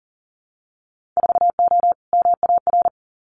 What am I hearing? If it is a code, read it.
Morse code, heard as a keyed tone, about 40 wpm, 700 Hz. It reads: 4O MAR
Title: Winter Field Day Exchanges Encore 40wpm